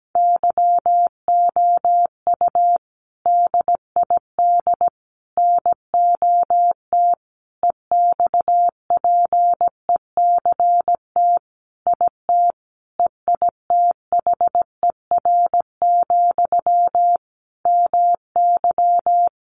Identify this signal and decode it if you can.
Morse code, heard as a keyed tone, about 17 wpm, 700 Hz. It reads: YOU DID NOT EXPECT IT EITHER, MY
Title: Sentences from Top 1100 Words 17wpm